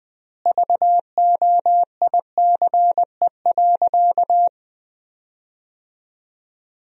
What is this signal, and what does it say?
Morse code, heard as a keyed tone, about 20 wpm, 700 Hz. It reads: VOICE.
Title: 4x Word-Spacing - Sentences from Top 500 Words - 20wpm